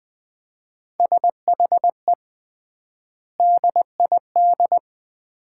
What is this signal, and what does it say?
Morse code, heard as a keyed tone, about 20 wpm, 700 Hz. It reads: SHE DID
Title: 3x Word-Spacing - Sets of 5 Words - 20wpm